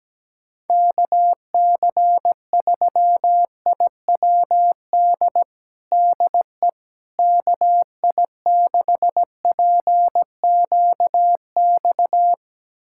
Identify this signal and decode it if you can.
Morse code, heard as a keyed tone, about 17 wpm, 700 Hz. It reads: KC3IWD DE KI6PQX
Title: USA Callsign DE Callsign - 6 Characters - 17wpm